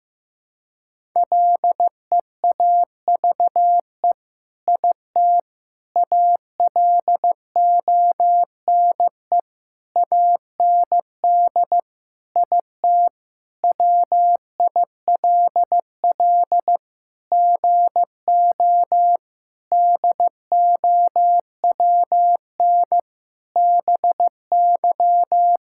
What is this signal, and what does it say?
Morse code, heard as a keyed tone, about 16 wpm, 700 Hz. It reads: LEAVE IT ALONE AND IT WILL GO DOWN BY
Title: Sentences from Top 1000 Words - Review Part 5 - 15wpm